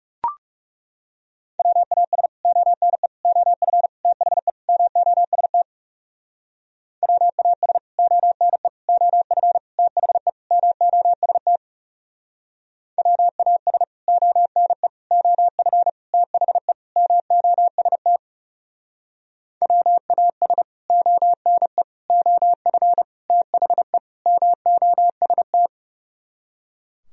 Morse code, heard as a keyed tone, about 36 wpm, 700 Hz. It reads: WAS ONE OF THE MOST WAS ONE OF THE MOST WAS ONE OF THE MOST WAS ONE OF THE MOST
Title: Speed-Racing - Sets of 5 Words 30wpm